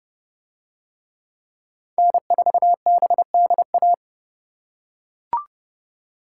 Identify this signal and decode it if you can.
Morse code, heard as a keyed tone, about 30 wpm, 700 Hz. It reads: N4BDA